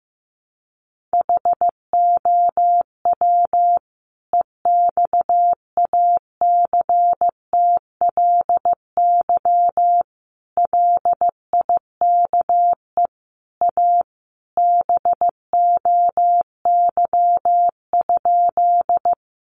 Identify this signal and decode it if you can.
Morse code, heard as a keyed tone, about 15 wpm, 700 Hz. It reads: HOW EXACTLY LIKE A BOY?
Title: Sentences from Top 900 Words 15wpm